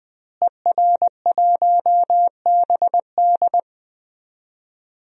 Morse code, heard as a keyed tone, about 20 wpm, 700 Hz. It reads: ER1BD
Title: DXCC-100 Call Signs - 5 Characters - Repeated 3x - 20wpm